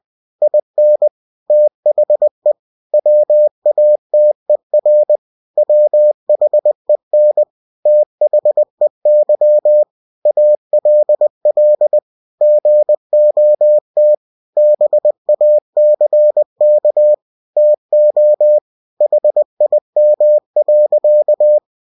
Morse code, heard as a keyed tone, about 20 wpm, 600 Hz. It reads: IN THE WATER WHEN THEY ALL GOT BACK TO HIM.
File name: cw-books / winnie-the-pooh / winnie-20wpm-page-0087